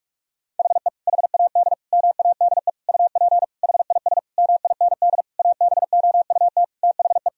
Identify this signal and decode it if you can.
Morse code, heard as a keyed tone, about 41 wpm, 700 Hz. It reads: HE HAD MADE UP HIS MIND ABOUT THE
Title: Sentences from Top 1000 Words - Review Part 3 - 45wpm